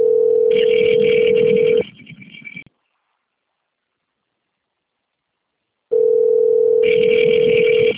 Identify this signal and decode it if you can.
Ringback tone (North America), 440 + 480 Hz.
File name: out- - - - - .27638